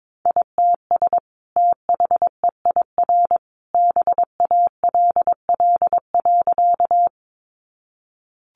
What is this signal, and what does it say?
Morse code, heard as a keyed tone, about 22 wpm, 700 Hz. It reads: ITS THEIR BALL.
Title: Sentences from Top 900 Words 22wpm